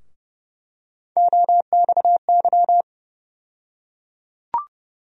Morse code, heard as a keyed tone, about 31 wpm, 700 Hz. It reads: OXY
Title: USA Call Sign Postfixes - 3 Letters 30wpm